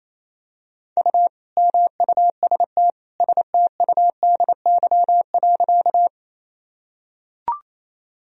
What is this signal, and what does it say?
Morse code, heard as a keyed tone, about 28 wpm, 700 Hz. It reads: U MUST STUDY.
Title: Sentences from Top 300 Words and Abbreviations 28wpm